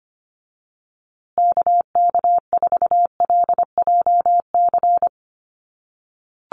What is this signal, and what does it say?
Morse code, heard as a keyed tone, about 25 wpm, 700 Hz. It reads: KK4LJC
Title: USA Call Signs Repeated 3x 25wpm